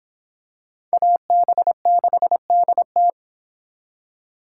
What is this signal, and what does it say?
Morse code, heard as a keyed tone, about 24 wpm, 700 Hz. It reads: AB6DT